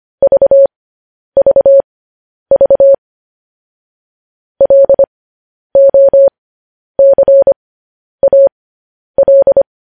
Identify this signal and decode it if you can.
Morse code, heard as a keyed tone, about 25 wpm, spaced as for 10 wpm, 550 Hz. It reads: VVV LOCAL
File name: podcast / News headlines-1x-25x10